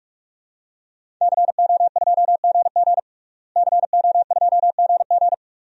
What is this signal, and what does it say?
Morse code, heard as a keyed tone, about 46 wpm, 700 Hz. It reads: CO2GG CO2GG
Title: DXCC-100 Call Signs - 5 Characters - Repeated 3x - 45wpm